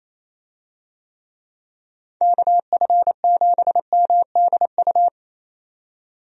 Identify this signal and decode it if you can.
Morse code, heard as a keyed tone, about 28 wpm, 700 Hz. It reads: KF7MDU
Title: USA Call Signs - 6 Characters - 28wpm